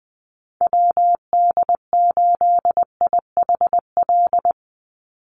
Morse code, heard as a keyed tone, about 20 wpm, 700 Hz. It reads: WD8IHL
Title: USA Call Signs 20wpm